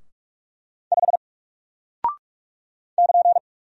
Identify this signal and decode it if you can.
Morse code, heard as a keyed tone, about 44 wpm, 700 Hz. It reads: 5 <KN>
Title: Single Character (Letter:Number:Punctuation:Prosign) Alt 45wpm